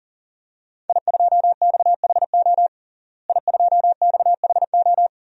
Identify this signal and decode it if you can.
Morse code, heard as a keyed tone, about 37 wpm, 700 Hz. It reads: I2XHO I2XHO
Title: DXCC-100 Call Signs - 5 Characters - Repeated 3x - 40wpm